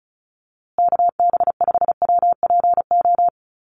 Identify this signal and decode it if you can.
Morse code, heard as a keyed tone, about 34 wpm, 700 Hz. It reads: KB5WPO
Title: USA Call Signs Repeated 2x 35wpm